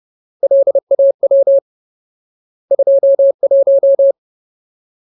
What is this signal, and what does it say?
Morse code, heard as a keyed tone, about 30 wpm, 550 Hz. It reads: LAW 21
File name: Ultimate Word List / 30WPM Ultimate